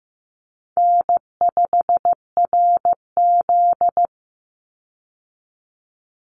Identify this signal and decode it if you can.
Morse code, heard as a keyed tone, about 15 wpm, 700 Hz. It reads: N5RZ
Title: Easy Call Signs Repeated 3x - 15wpm